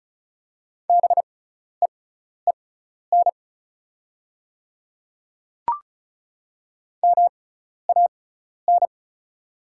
Morse code, heard as a keyed tone, about 34 wpm, spaced as for 12 wpm, 700 Hz. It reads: BEEN MAN
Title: 6x Character Spacing - Top 200 Words - 35wpm